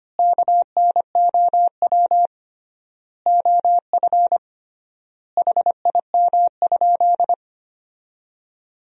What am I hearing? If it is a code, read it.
Morse code, heard as a keyed tone, about 25 wpm, 700 Hz. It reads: KNOW OF HIM?
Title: 3x Word-Spacing - Sentences from Top 100 Words - 25wpm